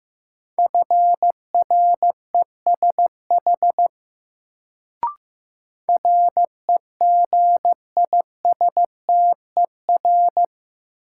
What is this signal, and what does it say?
Morse code, heard as a keyed tone, about 15 wpm, 700 Hz. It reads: FRESH REGISTER
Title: Sentences from Top 1200 Words - New Word Drill Alt - 15wpm